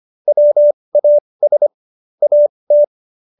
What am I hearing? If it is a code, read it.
Morse code, heard as a keyed tone, about 25 wpm, spaced as for 20 wpm, 590 Hz. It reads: WAS AT